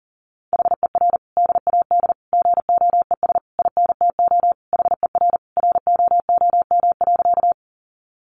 Morse code, heard as a keyed tone, about 40 wpm, 700 Hz. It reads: HER DAD GOES INTO HER ROOM.
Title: Additional Easy Phrases 40wpm